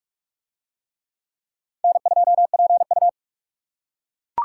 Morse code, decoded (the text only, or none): N2PU